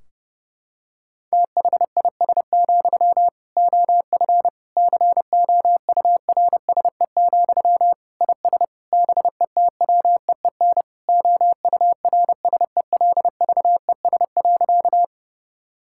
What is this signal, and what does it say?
Morse code, heard as a keyed tone, about 30 wpm, 700 Hz. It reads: THIS, OF COURSE, IS BETWEEN OURSELVES.
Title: Sentences from Top 1600 Words 30wpm